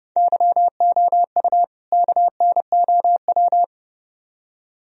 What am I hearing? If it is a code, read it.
Morse code, heard as a keyed tone, about 29 wpm, 700 Hz. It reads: YOU KNOW